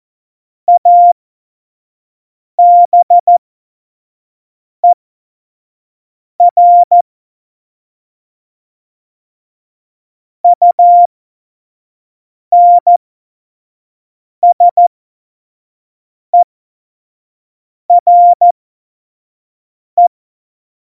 Morse code, heard as a keyed tone, about 14 wpm, spaced as for 5 wpm, 700 Hz. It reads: ABER UNSERE
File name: Morsekurs / Witze / Witz 19. Januar 2021, 25 F70